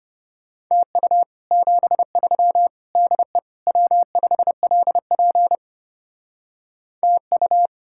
Morse code, heard as a keyed tone, about 33 wpm, 700 Hz. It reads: TU 73 DE W5LP TU